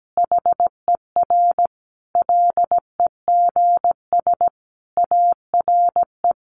Morse code, heard as a keyed tone, about 17 wpm, 700 Hz. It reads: HER LEGS ARE
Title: Easy Phrases 17wpm